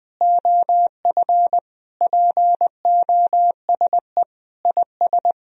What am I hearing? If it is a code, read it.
Morse code, heard as a keyed tone, about 21 wpm, 700 Hz. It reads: OF POSE IS
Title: Sentences from Top 1000 Words - Review Part 2 - 20wpm